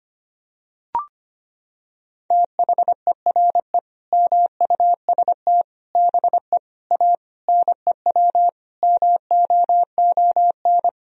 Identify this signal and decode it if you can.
Morse code, heard as a keyed tone, about 25 wpm, 700 Hz. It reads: THERE MUST BE A NEW MOON
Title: Sentences from Top 1000 Words - Review Part 3 - 25wpm